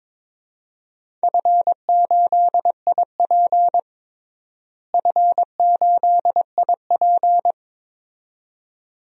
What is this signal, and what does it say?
Morse code, heard as a keyed tone, about 22 wpm, 700 Hz. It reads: F8IP F8IP
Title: DXCC-100 Call Signs - 4 Characters - Repeated 2x - 22wpm